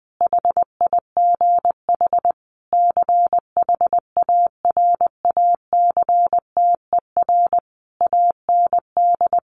Morse code, heard as a keyed tone, about 20 wpm, 700 Hz. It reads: HIGH CHARACTER AND
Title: Sentences from Top 1200 Words 20wpm